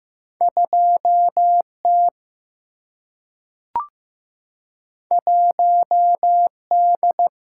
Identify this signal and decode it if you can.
Morse code, heard as a keyed tone, about 15 wpm, 700 Hz. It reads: 2 T 1 D
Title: Single-Letter Single-Number 15wpm